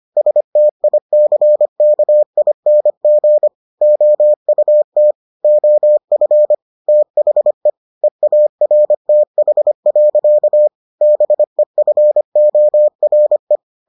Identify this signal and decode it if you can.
Morse code, heard as a keyed tone, about 25 wpm, 600 Hz. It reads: STICKING OUT OF THE EARTH. BEFORE